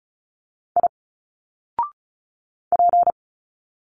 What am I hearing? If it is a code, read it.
Morse code, heard as a keyed tone, about 34 wpm, 700 Hz. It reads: I P
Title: Single Letters 35wpm